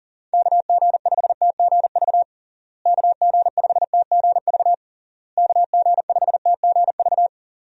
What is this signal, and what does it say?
Morse code, heard as a keyed tone, about 40 wpm, 700 Hz. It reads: KG5TGV KG5TGV KG5TGV
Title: USA Call Signs - 6 Characters - Repeated 3x - 40wpm